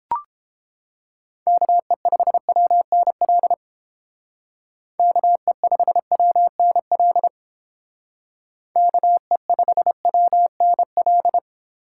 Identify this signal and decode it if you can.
Morse code, heard as a keyed tone, about 28 wpm, 700 Hz. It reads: KE5WNL KE5WNL KE5WNL